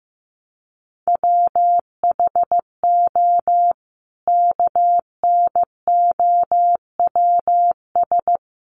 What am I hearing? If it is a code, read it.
Morse code, heard as a keyed tone, about 15 wpm, 700 Hz. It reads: WHO KNOWS
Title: A Sound Of Thunder 15wpm